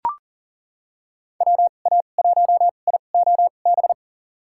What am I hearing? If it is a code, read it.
Morse code, heard as a keyed tone, about 40 wpm, spaced as for 29 wpm, 700 Hz. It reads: WA1IOB
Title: USA Call Signs - 6 Characters - 2x Character Spacing - 40wpm